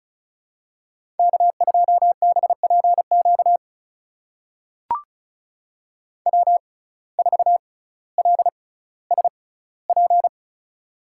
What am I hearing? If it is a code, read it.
Morse code, heard as a keyed tone, about 36 wpm, spaced as for 18 wpm, 700 Hz. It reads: K2BPQ W4LSP
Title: USA Call Signs - 5 Characters - 6x Character Spacing - 35wpm